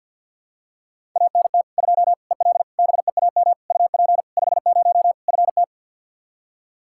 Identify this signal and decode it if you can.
Morse code, heard as a keyed tone, about 53 wpm, 700 Hz. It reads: ANT 2 EL BEAM UP 50 FT
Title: Practice QSO - Exchange 2 - Info - 50wpm